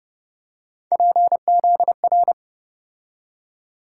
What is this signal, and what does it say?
Morse code, heard as a keyed tone, about 29 wpm, 700 Hz. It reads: PZR